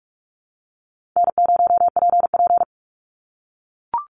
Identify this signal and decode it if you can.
Morse code, heard as a keyed tone, about 44 wpm, 700 Hz. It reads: N0PP
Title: Call Signs - 45wpm